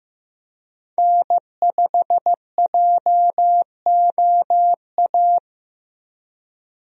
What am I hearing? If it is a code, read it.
Morse code, heard as a keyed tone, about 16 wpm, 700 Hz. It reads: N5JOA
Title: USA Call Signs - 5 Characters - Repeated 3x - 15wpm